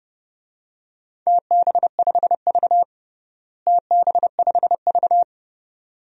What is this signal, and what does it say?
Morse code, heard as a keyed tone, about 30 wpm, 700 Hz. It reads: TB5V TB5V
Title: DXCC-100 Call Signs - 4 Characters - Repeated 2x - 30wpm